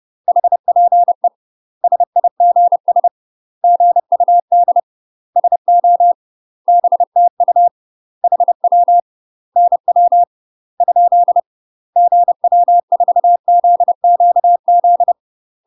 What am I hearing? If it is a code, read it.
Morse code, heard as a keyed tone, about 30 wpm, 700 Hz. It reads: HPE SIGS GUD SO BTU HW NW ? GW4ZQZ
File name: UK MASTER QSOs TRAINING / 30WPM / UK MASTER QSOs 0018